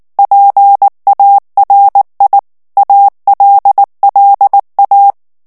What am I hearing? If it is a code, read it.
Morse code, heard as a keyed tone, about 19 wpm, 800 Hz. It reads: PARI ALLA
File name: morse mp3 / ITALIEN texte clair 2 vitesse 18 mots minute